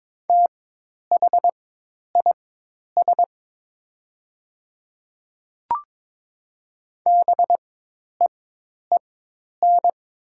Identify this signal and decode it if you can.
Morse code, heard as a keyed tone, about 22 wpm, spaced as for 10 wpm, 690 Hz. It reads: THIS BEEN
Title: 4x Character Spacing - Top 100 Words - 22wpm